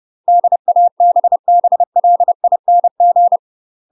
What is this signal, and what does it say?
Morse code, heard as a keyed tone, about 30 wpm, 700 Hz. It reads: DABBLING
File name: UK MASTER QSOs TRAINING / 30WPM / UK MASTER QSOs 0021